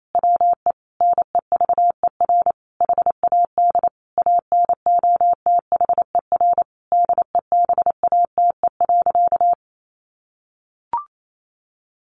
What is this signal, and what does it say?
Morse code, heard as a keyed tone, about 28 wpm, 700 Hz. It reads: WE NEVER HAD ANOTHER DEBATE.
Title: Sentences from Top 1100 Words 28wpm